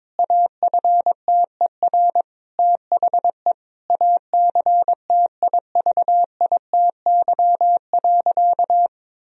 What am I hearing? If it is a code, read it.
Morse code, heard as a keyed tone, about 22 wpm, 700 Hz. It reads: AFTER THE ACTIVITY.
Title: Sentences from Top 1000 Words - Review Part 4 - 22wpm